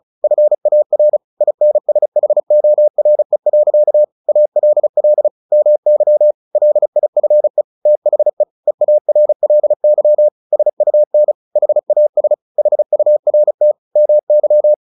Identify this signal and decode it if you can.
Morse code, heard as a keyed tone, about 34 wpm, 600 Hz. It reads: FAR INSHORE. ALL MY LIFE THE EARLY SUN HAS HURT MY
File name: the old man and the sea / the old man and the sea 1 35wpm